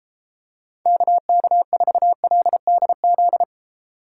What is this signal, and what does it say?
Morse code, heard as a keyed tone, about 32 wpm, 700 Hz. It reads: KK4LDZ